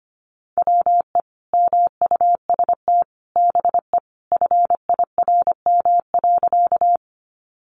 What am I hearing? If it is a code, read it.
Morse code, heard as a keyed tone, about 25 wpm, 700 Hz. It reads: WE MUST BE FIRM.
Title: Sentences from Top 1400 Words 25wpm